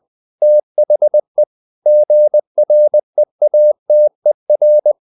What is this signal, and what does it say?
Morse code, heard as a keyed tone, about 20 wpm, 600 Hz. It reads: THE GREATER
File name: the old man and the sea part 2 20wpm